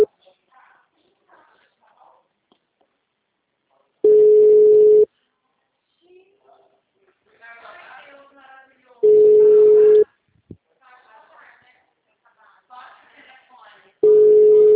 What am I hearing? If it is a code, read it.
Ringback tone (Europe, 425 Hz), with speech.